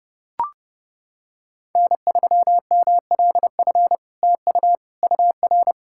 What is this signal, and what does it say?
Morse code, heard as a keyed tone, about 30 wpm, 700 Hz. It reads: N3MLF TU UR
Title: SOTA QSO Practice 30wpm